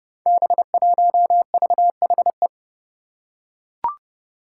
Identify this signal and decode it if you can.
Morse code, heard as a keyed tone, about 30 wpm, 700 Hz. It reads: B1VHE
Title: Five Challenging Letter-Number Combinations 30wpm